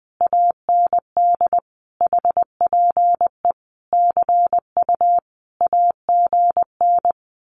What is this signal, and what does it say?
Morse code, heard as a keyed tone, about 20 wpm, 700 Hz. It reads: AND HPE CU AGN
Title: Common QSO Phrases Encore 20wpm